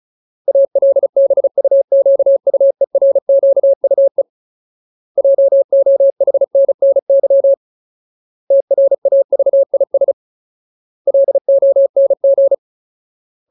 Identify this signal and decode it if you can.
Morse code, heard as a keyed tone, about 34 wpm, 550 Hz. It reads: ALBUQUERQUE JOHNNY TRAVIS LONG
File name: Ultimate Word List / 35WPM Ultimate